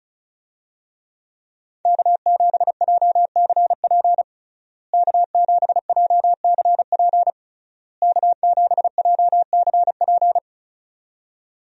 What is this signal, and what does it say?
Morse code, heard as a keyed tone, about 36 wpm, 700 Hz. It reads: K7JCP K7JCP K7JCP
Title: USA Call Signs - 5 Characters - Repeated 3x - 35wpm